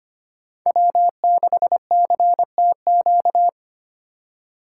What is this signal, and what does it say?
Morse code, heard as a keyed tone, about 25 wpm, 700 Hz. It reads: W6CTQ
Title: USA Call Signs - 5 Characters - 4x Character Spacing - 25wpm